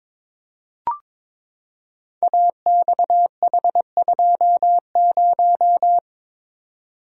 Morse code, heard as a keyed tone, about 22 wpm, 700 Hz. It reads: AXH20
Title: Code Groups - 5 Letter:Numbers - 22wpm